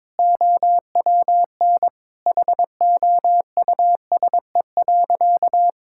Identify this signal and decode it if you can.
Morse code, heard as a keyed tone, about 22 wpm, 700 Hz. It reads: OWN HOUSE.
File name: practice / 22wpm / Sentences with 2-Word Phrases - 22wpm